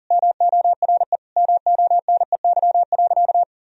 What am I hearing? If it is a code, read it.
Morse code, heard as a keyed tone, about 40 wpm, 700 Hz. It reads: MORE MONEY.
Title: Sentences from Top 1000 Words - Review Part 4 - 40wpm